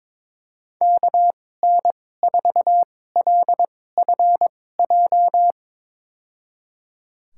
Morse code, heard as a keyed tone, about 22 wpm, spaced as for 16 wpm, 700 Hz. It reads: KN4LFJ
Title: USA Call Signs - 2x Character Spacing - 22wpm